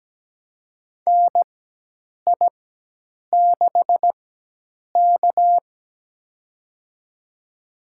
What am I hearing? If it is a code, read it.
Morse code, heard as a keyed tone, about 17 wpm, spaced as for 8 wpm, 700 Hz. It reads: NI6K
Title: USA Call Signs - 4 Characters - 4x Character Spacing - 17wpm